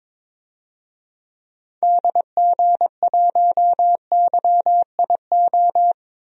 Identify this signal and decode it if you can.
Morse code, heard as a keyed tone, about 22 wpm, 700 Hz. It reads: DG1YIO
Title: DXCC-100 Call Signs - 6 Characters - Repeated 2x - 22wpm